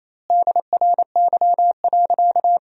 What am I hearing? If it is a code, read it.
Morse code, heard as a keyed tone, about 28 wpm, 700 Hz. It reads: DRY.
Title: 2x Word-Spacing - Sentences from Top 500 Words - 28wpm